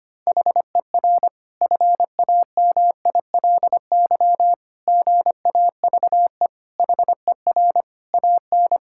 Morse code, heard as a keyed tone, about 25 wpm, 700 Hz. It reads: HER FAMILY GAVE HER AN
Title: Sentences from Top 1000 Words - Review Part 5 - 25wpm